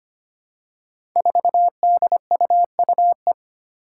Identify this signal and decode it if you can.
Morse code, heard as a keyed tone, about 25 wpm, 700 Hz. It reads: 4DUUE
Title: Five Challenging Letter-Number Combinations 25wpm